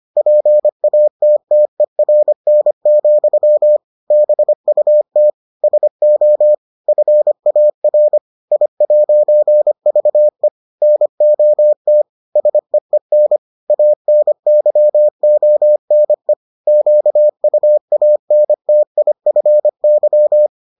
Morse code, heard as a keyed tone, about 25 wpm, 600 Hz. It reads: PATTERN, BUT SO FAR I'VE NOT SEEN ANYONE QUANTIFY